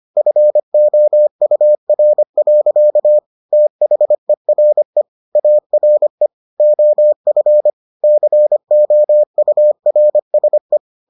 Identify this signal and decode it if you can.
Morse code, heard as a keyed tone, about 25 wpm, 600 Hz. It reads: FOUR. THERE ARE OF COURSE